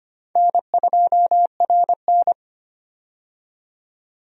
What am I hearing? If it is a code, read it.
Morse code, heard as a keyed tone, about 25 wpm, 700 Hz. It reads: N2RN
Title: USA Call Signs - 4 Characters - 25wpm